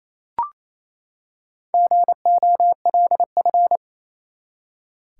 Morse code, heard as a keyed tone, about 28 wpm, 700 Hz. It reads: GOLF
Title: Four Letter Words - 28wpm